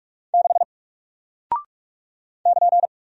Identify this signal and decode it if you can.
Morse code, heard as a keyed tone, about 44 wpm, 700 Hz. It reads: 6 <KN>
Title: Single Character (Letter:Number:Punctuation:Prosign) Alt 45wpm